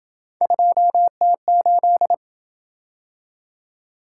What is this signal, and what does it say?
Morse code, heard as a keyed tone, about 27 wpm, 700 Hz. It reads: 2T8